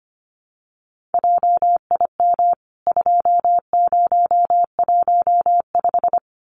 Morse code, heard as a keyed tone, about 25 wpm, 700 Hz. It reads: JIM 2015
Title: CWT Exchange 25wpm